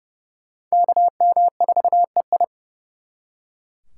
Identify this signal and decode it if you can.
Morse code, heard as a keyed tone, about 36 wpm, 700 Hz. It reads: KM4EI
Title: Speed Racing - USA 2x2 Call Signs 30wpm